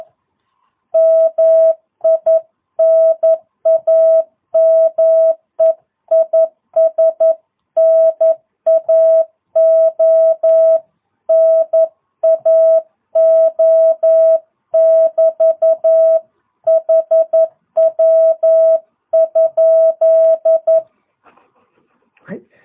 Morse code, heard as a keyed tone, about 11 wpm, 660 Hz. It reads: MI NAME IS NAO NAO <BT> HW?
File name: jh8cbh / 2022cw